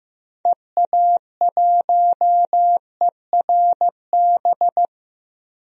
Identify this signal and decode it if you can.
Morse code, heard as a keyed tone, about 16 wpm, 700 Hz. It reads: EA1ERB
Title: DXCC-100 Call Signs - 6 Characters - Repeated 3x - 15wpm